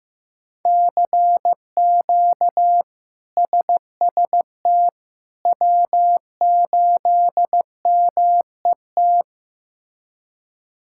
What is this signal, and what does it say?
Morse code, heard as a keyed tone, about 15 wpm, 700 Hz. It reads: CQ SST W8MET
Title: K1USN SST QSO Practice - 15wpm